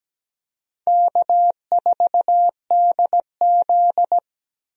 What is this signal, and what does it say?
Morse code, heard as a keyed tone, about 17 wpm, 700 Hz. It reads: K4DZ
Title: USA Call Signs - 4 Characters - Repeated 2x - 17wpm